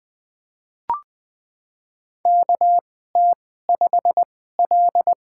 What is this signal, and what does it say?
Morse code, heard as a keyed tone, about 20 wpm, spaced as for 15 wpm, 700 Hz. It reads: KT5L
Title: USA Call Signs - 4 Characters - 2x Character Spacing - 20wpm